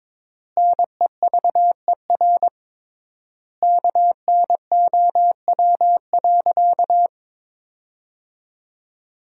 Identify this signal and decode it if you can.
Morse code, heard as a keyed tone, about 22 wpm, 700 Hz. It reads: NEVER KNOW.
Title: 3x Word-Spacing - Sentences from Top 300 Words - 22wpm